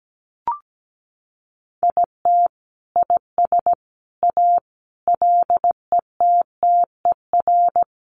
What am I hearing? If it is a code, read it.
Morse code, heard as a keyed tone, about 17 wpm, 700 Hz. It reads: IT IS A LETTER
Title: Sentences from Top 1400 Words 17wpm